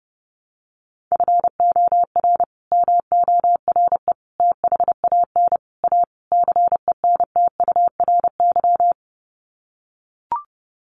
Morse code, heard as a keyed tone, about 30 wpm, 700 Hz. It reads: FOR MORE THAN A CENTURY
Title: Sets of 5 Words Encore2 30wpm